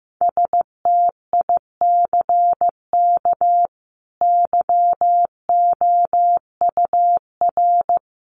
Morse code, heard as a keyed tone, about 15 wpm, 700 Hz. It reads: STICK YOUR
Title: A Sound Of Thunder 15wpm